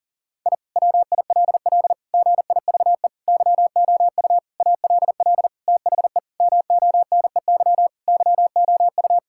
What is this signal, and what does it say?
Morse code, heard as a keyed tone, about 40 wpm, 700 Hz. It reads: I WILL GIVE YOU ALL THE MONEY YOU
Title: Sentences from Top 1000 Words - Review Part 6 - 40wpm